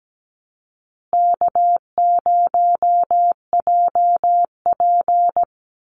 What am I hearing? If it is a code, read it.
Morse code, heard as a keyed tone, about 17 wpm, 700 Hz. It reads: K0JP
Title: Call Signs Repeated 2x - 17wpm